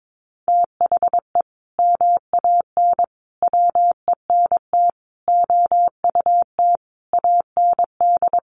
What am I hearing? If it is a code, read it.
Morse code, heard as a keyed tone, about 22 wpm, 700 Hz. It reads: THE MAN WENT OUT AND
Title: Sentences from Top 1400 Words 22wpm